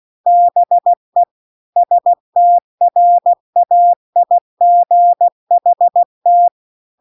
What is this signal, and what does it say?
Morse code, heard as a keyed tone, about 16 wpm, 700 Hz. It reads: BE STRAIGHT